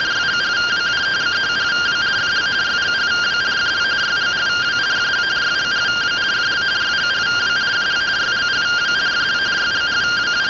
Two-tone teleprinter signal, about 170 Hz shift.